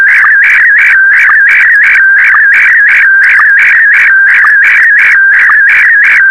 SSTV (slow-scan television), Scottie DX.
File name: sstv / SDX-VK4MSL